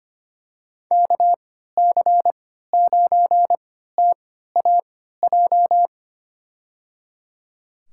Morse code, heard as a keyed tone, about 25 wpm, spaced as for 14 wpm, 700 Hz. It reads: KC9TAJ
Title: USA Call Signs - 3x Character Spacing - 25wpm